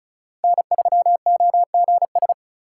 Morse code, heard as a keyed tone, about 45 wpm, 700 Hz. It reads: N3OGS